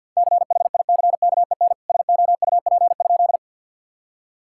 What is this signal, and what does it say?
Morse code, heard as a keyed tone, about 49 wpm, 700 Hz. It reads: CHICKEN SOUP?
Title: Sentences from Top 1000 Words - Review Part 6 - 50wpm